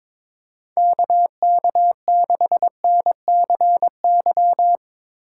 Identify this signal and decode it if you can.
Morse code, heard as a keyed tone, about 22 wpm, 700 Hz. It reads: KK6NCY
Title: Speed Racing - USA 2x3 Call Signs 22wpm